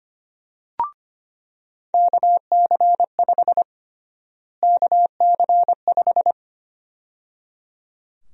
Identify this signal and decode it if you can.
Morse code, heard as a keyed tone, about 25 wpm, 700 Hz. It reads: KC5 KC5
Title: USA Call Sign Prefixes - Repeated 2x - 25wpm